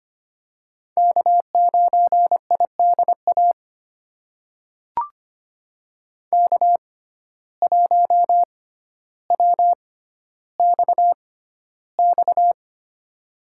Morse code, heard as a keyed tone, about 25 wpm, spaced as for 13 wpm, 700 Hz. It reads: K9IDA K1WXX
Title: USA Call Signs - 5 Characters - 6x Character Spacing - 25wpm